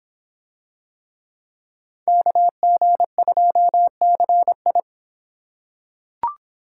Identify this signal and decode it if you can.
Morse code, heard as a keyed tone, about 26 wpm, 700 Hz. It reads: KG2CI